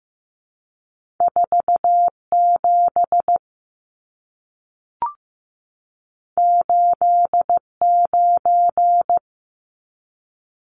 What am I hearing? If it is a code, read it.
Morse code, heard as a keyed tone, about 15 wpm, 700 Hz. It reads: 47 89
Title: Numbers - 2-Digit 15wpm